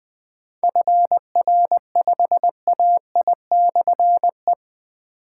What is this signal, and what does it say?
Morse code, heard as a keyed tone, about 20 wpm, 700 Hz. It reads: FR5AI/E
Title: Difficult Call Signs 20wpm